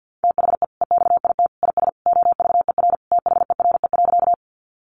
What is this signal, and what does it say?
Morse code, heard as a keyed tone, about 50 wpm, 700 Hz. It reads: THE EXIT IS OVER THERE.
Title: Sentences from Top 1100 Words 50wpm